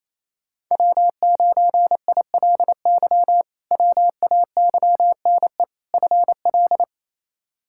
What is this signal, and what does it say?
Morse code, heard as a keyed tone, about 28 wpm, 700 Hz. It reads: W9ILY WAYNE FL
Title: K1USN SST QSO Practice - 28wpm